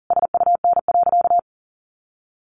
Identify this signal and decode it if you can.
Morse code, heard as a keyed tone, about 40 wpm, 700 Hz. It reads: SUN.
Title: Sentences from Top 500 Words 40wpm